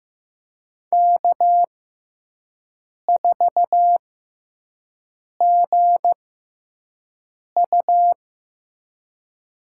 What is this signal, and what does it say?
Morse code, heard as a keyed tone, about 15 wpm, spaced as for 5 wpm, 700 Hz. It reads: K4GU
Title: USA Call Signs - 6x Character Spacing - 15wpm